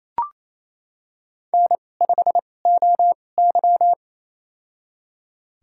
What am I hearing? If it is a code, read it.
Morse code, heard as a keyed tone, about 27 wpm, spaced as for 20 wpm, 700 Hz. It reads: N5OY
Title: USA Call Signs - 4 Characters - 2x Character Spacing - 28wpm